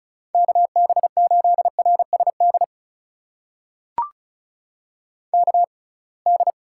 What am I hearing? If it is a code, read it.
Morse code, heard as a keyed tone, about 36 wpm, spaced as for 27 wpm, 700 Hz. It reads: KB8RSD KD
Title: USA Call Signs - 6 Characters - 6x Character Spacing - 35wpm